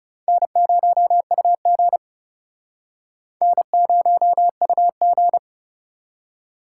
Morse code, heard as a keyed tone, about 37 wpm, 700 Hz. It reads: N0UG N0UG
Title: Speed Racing - USA 1x2 Call Signs 30wpm